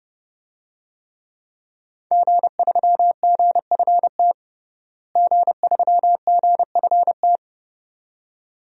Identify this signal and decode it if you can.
Morse code, heard as a keyed tone, about 29 wpm, 700 Hz. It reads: G3GFT G3GFT
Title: DXCC-100 Call Signs - 5 Characters - Repeated 2x - 30wpm